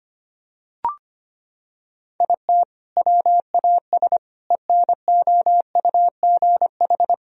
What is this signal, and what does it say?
Morse code, heard as a keyed tone, about 25 wpm, 700 Hz. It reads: IT WAS ENOUGH
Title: Sentences from Top 1000 Words - Review Part 4 - 25wpm